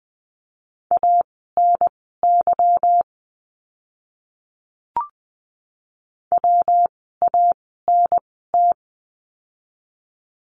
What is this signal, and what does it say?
Morse code, heard as a keyed tone, about 20 wpm, spaced as for 15 wpm, 700 Hz. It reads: ANY WANT
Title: 160 Lesson 26f - Letter g - Words 20wpm